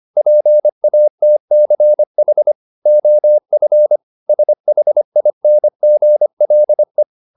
Morse code, heard as a keyed tone, about 25 wpm, 600 Hz. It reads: PATCH OF SHINGLE